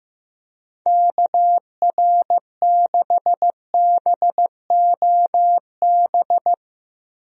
Morse code, heard as a keyed tone, about 15 wpm, 700 Hz. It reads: KR6BOB
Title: USA Call Signs - 6 Characters - Repeated 3x - 15wpm